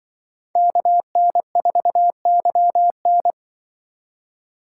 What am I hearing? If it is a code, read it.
Morse code, heard as a keyed tone, about 24 wpm, 700 Hz. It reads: KN4YN
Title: Speed Racing - USA Call Signs 20wpm